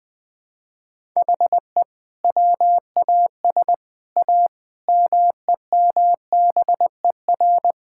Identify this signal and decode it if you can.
Morse code, heard as a keyed tone, about 21 wpm, 700 Hz. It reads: HE WAS A MEMBER
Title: Sets of 6 Words - 20wpm